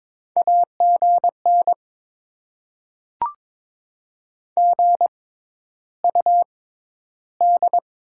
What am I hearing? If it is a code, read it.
Morse code, heard as a keyed tone, about 22 wpm, spaced as for 11 wpm, 690 Hz. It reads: AGN GUD
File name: Top 20 QSO Elements - 6x Character Spacing - 22wpm